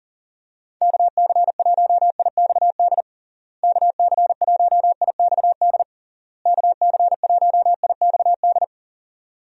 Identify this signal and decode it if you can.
Morse code, heard as a keyed tone, about 40 wpm, 700 Hz. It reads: KC1IXD KC1IXD KC1IXD
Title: USA Call Signs - 6 Characters - Repeated 3x - 40wpm